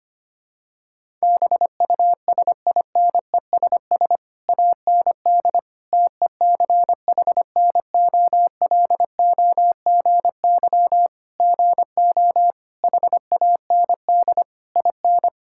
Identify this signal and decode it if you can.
Morse code, heard as a keyed tone, about 25 wpm, 700 Hz. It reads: BUSINESS AND TECHNOLOGY GO HAND IN
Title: Sentences from Top 1000 Words - Review Part 5 - 25wpm